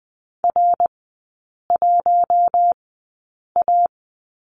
Morse code, heard as a keyed tone, about 20 wpm, spaced as for 8 wpm, 700 Hz. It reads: R1A
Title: 053 Lesson 10e - Letter r - Three Characters 20wpm